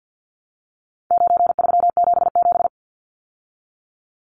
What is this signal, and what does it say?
Morse code, heard as a keyed tone, about 50 wpm, 700 Hz. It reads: 9377
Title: Numbers - 4-Digit 50wpm